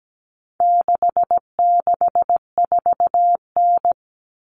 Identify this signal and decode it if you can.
Morse code, heard as a keyed tone, about 17 wpm, 700 Hz. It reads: 664N
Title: Common Cut Numbers - 4-Digit 17wpm